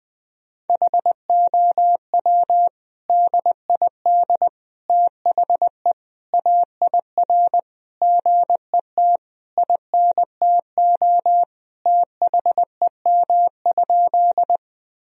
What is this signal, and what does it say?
Morse code, heard as a keyed tone, about 21 wpm, 700 Hz. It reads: HOW DID THE AIR GET INTO THEM?
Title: Sentences from Top 1000 Words - Review Part 1 - 20wpm